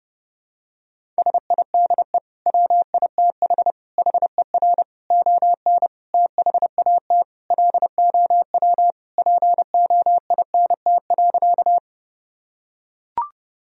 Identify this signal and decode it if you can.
Morse code, heard as a keyed tone, about 30 wpm, 700 Hz. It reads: SIDE WITH HER ON THAT LOW POINT.
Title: Sentences from Top 1000 Words - Review Part 1 - 30wpm